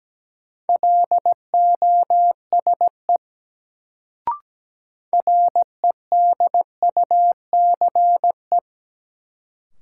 Morse code, heard as a keyed tone, about 17 wpm, 700 Hz. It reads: LOSE REDUCE
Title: Sentences from Top 1100 Words - New Word Drill Alt - 17wpm